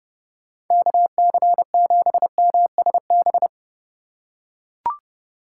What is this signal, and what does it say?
Morse code, heard as a keyed tone, about 30 wpm, 700 Hz. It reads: KC7MSB